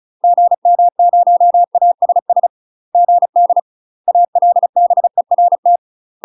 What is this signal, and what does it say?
Morse code, heard as a keyed tone, about 34 wpm, 700 Hz. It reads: GM0ASS GD ALBERT
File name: UK MASTER QSOs TRAINING / 35WPM / UK MASTER QSOs 0037